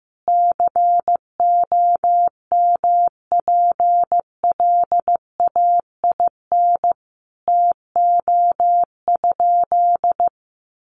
Morse code, heard as a keyed tone, about 15 wpm, 700 Hz. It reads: COMPLAIN TO?
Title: Sentences from Top 1300 Words 15wpm